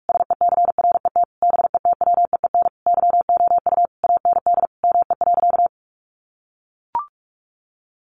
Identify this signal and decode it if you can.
Morse code, heard as a keyed tone, about 44 wpm, 700 Hz. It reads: SECRET BETWEEN YOU AND ME.
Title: Sentences from Top 1200 Words 45wpm